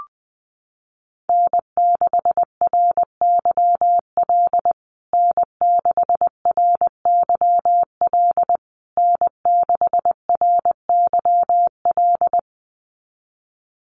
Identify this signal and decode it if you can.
Morse code, heard as a keyed tone, about 20 wpm, 700 Hz. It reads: N6RYL N6RYL N6RYL
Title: USA Call Signs Repeated 3x 20wpm